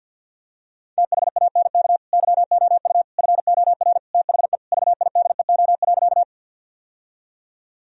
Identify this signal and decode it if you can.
Morse code, heard as a keyed tone, about 50 wpm, 700 Hz. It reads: THANK YOU FOR THE VIDEO.
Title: Sentences from Top 1000 Words - Review Part 4 - 50wpm